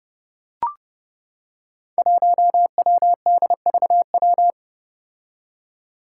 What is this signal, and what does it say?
Morse code, heard as a keyed tone, about 32 wpm, 700 Hz. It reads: 1WDVW
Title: Five Challenging Letter-Number Combinations 30wpm